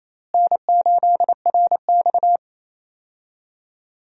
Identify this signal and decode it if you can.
Morse code, heard as a keyed tone, about 28 wpm, 700 Hz. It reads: N8RX
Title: USA Call Signs - 4 Characters - Repeated 3x - 28wpm